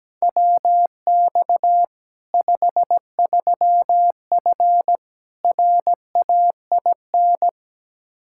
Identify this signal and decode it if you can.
Morse code, heard as a keyed tone, about 17 wpm, 700 Hz. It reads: WX 53F RAIN
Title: Practice QSO - Exchange 2 - Info - 17wpm